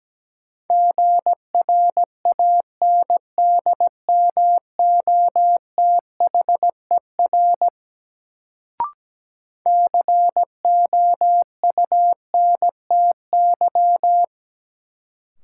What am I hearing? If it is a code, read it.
Morse code, heard as a keyed tone, about 17 wpm, 700 Hz. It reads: GRANDMOTHER COUNTY
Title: Sentences from Top 1200 Words - New Word Drill - 17wpm